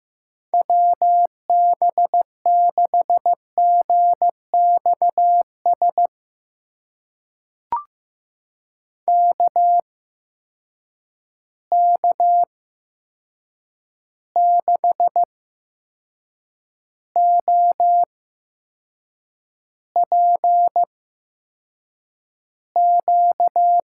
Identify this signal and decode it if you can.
Morse code, heard as a keyed tone, about 15 wpm, spaced as for 6 wpm, 700 Hz. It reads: WB6GXS KK6OPQ
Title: USA Call Signs - 6 Characters - 8x Character Spacing - 15wpm